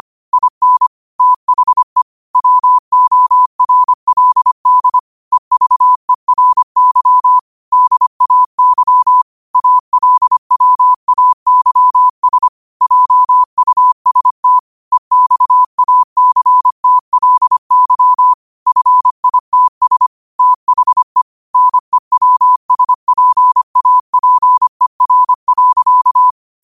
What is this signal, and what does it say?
Morse code, heard as a keyed tone, about 25 wpm, 1000 Hz. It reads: IN THE WORLD EVERY DAY ALWAYS JUST EXACTLY FITS THE NEWSPAPER.